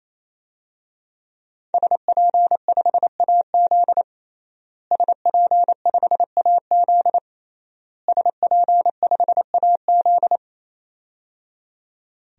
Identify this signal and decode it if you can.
Morse code, heard as a keyed tone, about 28 wpm, 700 Hz. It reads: SP5AZ SP5AZ SP5AZ
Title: DXCC-100 Call Signs - 5 Characters - Repeated 3x - 28wpm